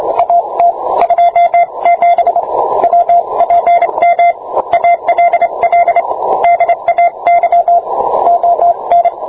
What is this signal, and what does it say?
Morse code, heard as a keyed tone, about 26 wpm, 690 Hz, with radio noise. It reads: AT 27 WPM ALL DAY ON